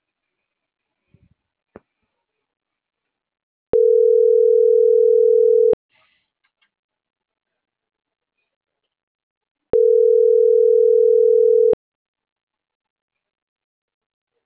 Ringback tone (North America), 440 + 480 Hz.